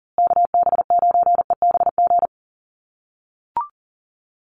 Morse code, heard as a keyed tone, about 40 wpm, 700 Hz. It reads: KB9EBG
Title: USA Call Signs Repeated 2x 40wpm